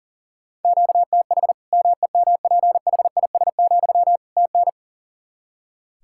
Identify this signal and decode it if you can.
Morse code, heard as a keyed tone, about 37 wpm, 700 Hz. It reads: QTH MEMPHIS, TN
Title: Practice QSO - Exchange 1 - Beginning - 40wpm